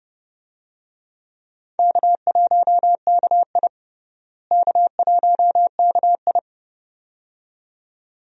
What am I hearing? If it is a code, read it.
Morse code, heard as a keyed tone, about 30 wpm, 700 Hz. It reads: K1KI K1KI
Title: USA Call Signs - 4 Characters - Repeated 2x - 30wpm